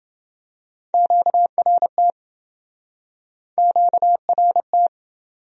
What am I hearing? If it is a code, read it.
Morse code, heard as a keyed tone, about 29 wpm, 700 Hz. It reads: QRT QRT